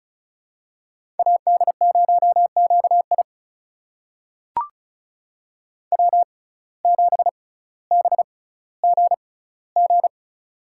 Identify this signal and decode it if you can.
Morse code, heard as a keyed tone, about 36 wpm, spaced as for 18 wpm, 700 Hz. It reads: AD0QI W7BGG
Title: USA Call Signs - 5 Characters - 6x Character Spacing - 35wpm